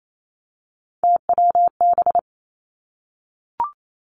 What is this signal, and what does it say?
Morse code, heard as a keyed tone, about 28 wpm, 700 Hz. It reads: TWB
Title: Three Challenging Letter-Number Combinations 28wpm